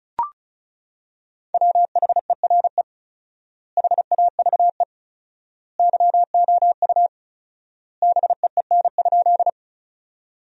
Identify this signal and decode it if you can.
Morse code, heard as a keyed tone, about 36 wpm, 700 Hz. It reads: WHERE HAVE YOU BEEN?
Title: 4x Word-Spacing - Sentences from Top 200 Words - 35wpm